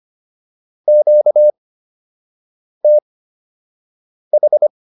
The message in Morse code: QTH

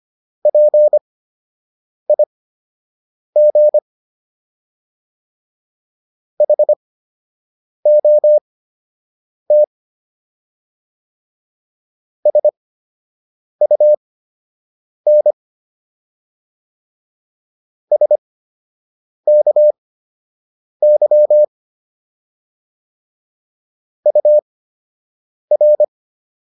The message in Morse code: PIG HOT SUN SKY UR

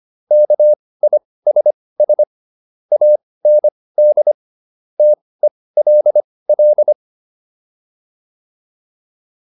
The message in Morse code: KISS AND TELL

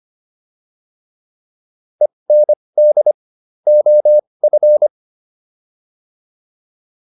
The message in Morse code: END OF

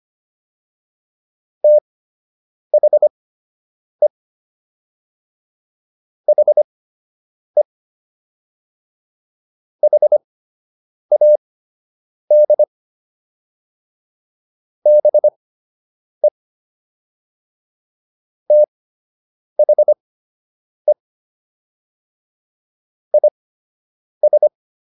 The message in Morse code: THE HE HAD BE THE IS

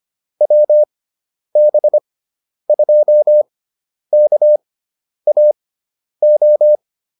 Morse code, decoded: WB2KAO

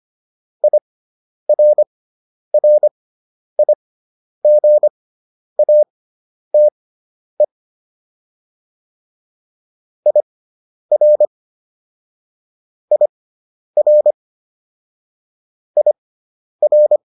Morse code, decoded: IRRIGATE IR IR IR

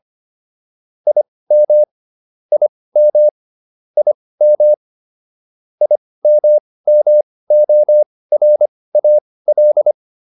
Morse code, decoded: IM IM IM IMMORAL

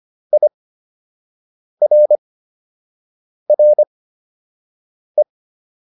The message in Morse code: IRRE

